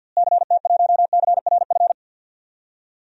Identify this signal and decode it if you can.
Morse code, heard as a keyed tone, about 52 wpm, 700 Hz. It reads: CT1CRF
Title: DXCC-100 Call Signs - 6 Characters - Repeated 2x - 50wpm